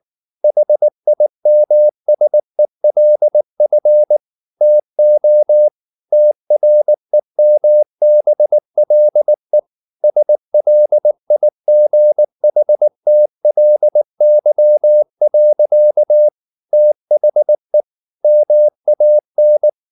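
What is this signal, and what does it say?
Morse code, heard as a keyed tone, about 19 wpm, 600 Hz. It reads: HIMSELF TO TREMBLE SLIGHTLY. THE MAN
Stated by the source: Morse code at 19 wpm from CW med Gnister, reading Ken Follett's novel Never in English.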